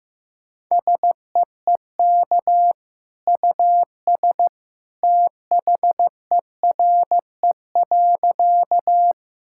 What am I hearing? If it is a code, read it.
Morse code, heard as a keyed tone, about 14 wpm, 700 Hz. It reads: SEEK US THERE.
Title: Sentences from Top 1600 Words 15wpm